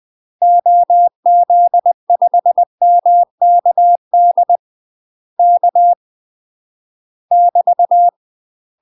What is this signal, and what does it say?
Morse code, heard as a keyed tone, about 20 wpm, 700 Hz. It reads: OZ5MKD K <BT>
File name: UK MASTER QSOs TRAINING / 20WPM / UK MASTER QSOs 0053